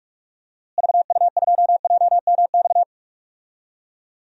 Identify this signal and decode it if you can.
Morse code, heard as a keyed tone, about 46 wpm, 700 Hz. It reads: VU2JMX
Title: DXCC-100 Call Signs - 6 Characters - Repeated 3x - 45wpm